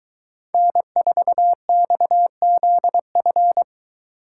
Morse code, heard as a keyed tone, about 23 wpm, 700 Hz. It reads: N4XZF